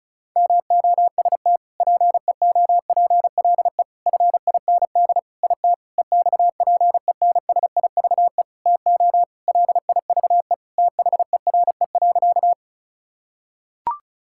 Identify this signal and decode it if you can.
Morse code, heard as a keyed tone, about 34 wpm, 700 Hz. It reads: MOST PEOPLE FIND IT EXPENSIVE TO LIVE THERE.
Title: Sentences from Top 1000 Words - Review Part 6 - 35wpm